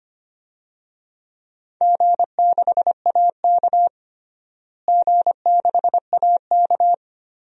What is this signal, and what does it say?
Morse code, heard as a keyed tone, about 25 wpm, 700 Hz. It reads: G6AK G6AK
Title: DXCC-100 Call Signs - 4 Characters - Repeated 2x - 25wpm